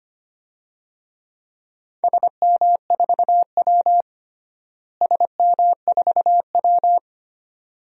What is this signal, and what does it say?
Morse code, heard as a keyed tone, about 25 wpm, 700 Hz. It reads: SM4W SM4W
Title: DXCC-100 Call Signs - 4 Characters - Repeated 2x - 25wpm